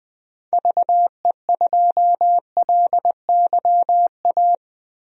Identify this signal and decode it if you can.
Morse code, heard as a keyed tone, about 20 wpm, 700 Hz. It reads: VE2LYA
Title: DXCC-100 Call Signs - 6 Characters - Repeated 2x - 20wpm